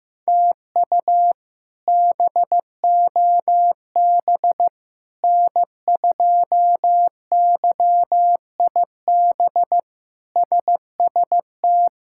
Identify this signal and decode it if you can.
Morse code, heard as a keyed tone, about 15 wpm, 700 Hz. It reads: TU BOB N2YIB SST
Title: K1USN SST QSO Practice - 15wpm